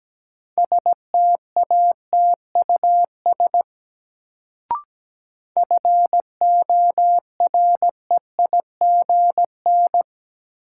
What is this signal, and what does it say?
Morse code, heard as a keyed tone, about 17 wpm, 700 Hz. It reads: STATUS FOREIGN
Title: Sentences from Top 1400 Words - New Word Drill - 17wpm